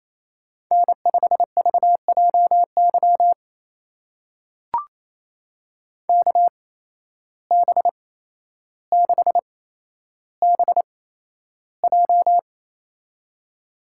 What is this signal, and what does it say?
Morse code, heard as a keyed tone, about 28 wpm, spaced as for 12 wpm, 700 Hz. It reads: N5VJY KB6BJ
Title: USA Call Signs - 5 Characters - 8x Character Spacing - 28wpm